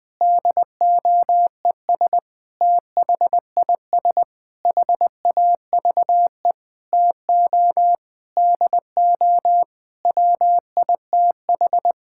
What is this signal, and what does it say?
Morse code, heard as a keyed tone, about 20 wpm, 700 Hz. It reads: DOES THIS HAVE TO DO WITH
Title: Sentences with 8-Word Phrases - 20wpm